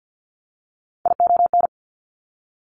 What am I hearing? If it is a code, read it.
Morse code, heard as a keyed tone, about 50 wpm, 700 Hz. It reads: ION
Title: Top Three Letter Combinations in Words 50wpm